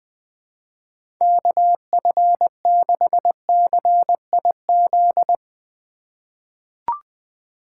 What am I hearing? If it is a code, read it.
Morse code, heard as a keyed tone, about 20 wpm, 700 Hz. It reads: KF6CIZ
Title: USA Call Signs - 6 Characters - 3x Character Spacing - 20wpm